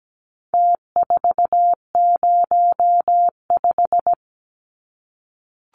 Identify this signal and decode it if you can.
Morse code, heard as a keyed tone, about 17 wpm, 700 Hz. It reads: T405
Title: Numbers - Contest Style - 4-Digit 17wpm